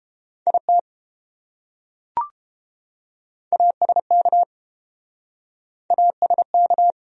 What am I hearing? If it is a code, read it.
Morse code, heard as a keyed tone, about 29 wpm, 700 Hz. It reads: IT ASK ASK